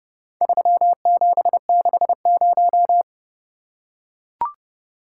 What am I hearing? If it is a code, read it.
Morse code, heard as a keyed tone, about 29 wpm, 700 Hz. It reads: 3760